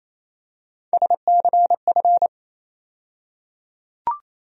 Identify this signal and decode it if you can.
Morse code, heard as a keyed tone, about 28 wpm, 700 Hz. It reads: SCF
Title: USA Call Sign Postfixes - 3 Letters 28wpm